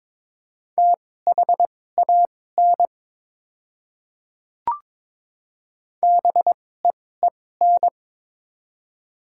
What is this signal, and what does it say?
Morse code, heard as a keyed tone, about 22 wpm, spaced as for 16 wpm, 700 Hz. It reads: THAN BEEN
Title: 2x Character Spacing - Top 100 Words - 22wpm